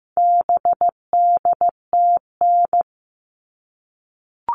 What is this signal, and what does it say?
Morse code, heard as a keyed tone, about 15 wpm, 700 Hz. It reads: BDTN
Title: All Cut Numbers - 4-Digit 15wpm